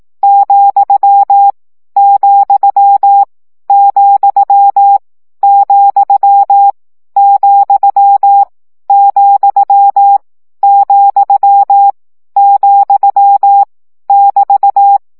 Morse code, heard as a keyed tone, about 18 wpm, 800 Hz. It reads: , , , , , , , , <BT>
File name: CW gyakorlas vesszo 15 baud